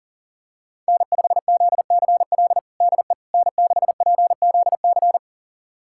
Morse code, heard as a keyed tone, about 40 wpm, 700 Hz. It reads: N5ZCL DE N6PZC
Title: USA Callsign DE Callsign 40wpm